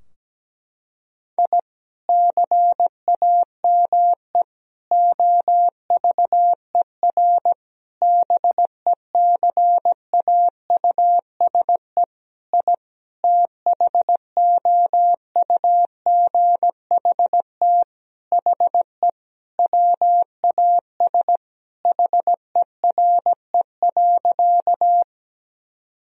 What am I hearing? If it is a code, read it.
Morse code, heard as a keyed tone, about 17 wpm, 700 Hz. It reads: I CAME OVER BECAUSE I THOUGHT HE WAS HERE.
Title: Sentences from Top 1000 Words - Review Part 3 - 17wpm